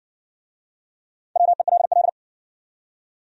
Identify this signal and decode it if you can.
Morse code, heard as a keyed tone, about 60 wpm, 700 Hz. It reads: WELL